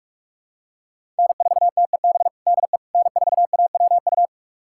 Morse code, heard as a keyed tone, about 43 wpm, 700 Hz. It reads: N4TEB DE N4AWU
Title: USA Callsign DE Callsign - 5 Characters - 45wpm